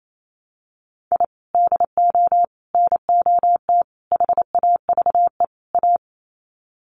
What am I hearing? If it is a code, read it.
Morse code, heard as a keyed tone, about 28 wpm, 700 Hz. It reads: I DO NOT HAVE A
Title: Sets of 5 Words 28wpm